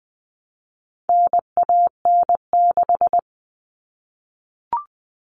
Morse code, heard as a keyed tone, about 20 wpm, 700 Hz. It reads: NAN6
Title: All Cut Numbers - 4-Digit 20wpm